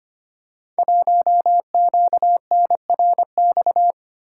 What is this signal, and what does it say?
Morse code, heard as a keyed tone, about 25 wpm, 700 Hz. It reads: 1QNRX
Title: Code Groups - 5 Letter:Numbers - 25wpm